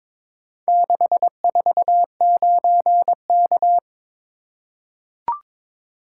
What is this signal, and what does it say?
Morse code, heard as a keyed tone, about 22 wpm, 700 Hz. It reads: 649K
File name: practice / 22wpm / Four Letter-Number Combinations - 22wpm